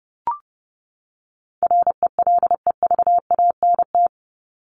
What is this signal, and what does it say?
Morse code, heard as a keyed tone, about 30 wpm, 700 Hz. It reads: RELEVANT
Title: Top 1001-1250 Words 30wpm